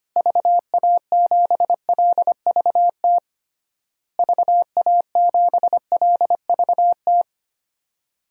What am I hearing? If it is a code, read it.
Morse code, heard as a keyed tone, about 25 wpm, 700 Hz. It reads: VA7LVT VA7LVT
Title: DXCC-100 Call Signs - 6 Characters - Repeated 3x - 25wpm